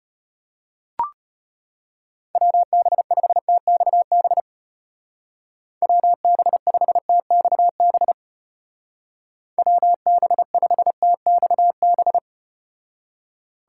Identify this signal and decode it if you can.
Morse code, heard as a keyed tone, about 33 wpm, 700 Hz. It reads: WB5TXB WB5TXB WB5TXB